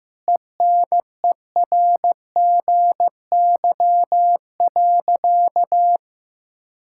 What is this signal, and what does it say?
Morse code, heard as a keyed tone, about 15 wpm, 700 Hz. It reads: ENERGY.
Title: Sentences from Top 1000 Words - Review Part 6 - 15wpm